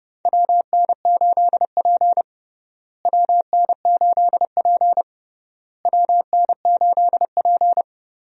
Morse code, heard as a keyed tone, about 30 wpm, 700 Hz. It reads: WN8P WN8P WN8P
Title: USA Call Signs - 4 Characters - Repeated 3x - 30wpm